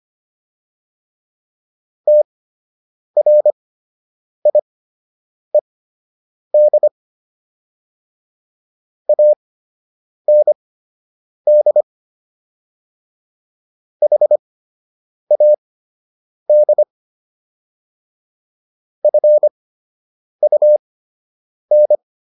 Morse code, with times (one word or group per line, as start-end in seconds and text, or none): 2.1-6.9 TRIED
9.1-11.8 AND
14.0-16.8 HAD
19.0-21.9 FUN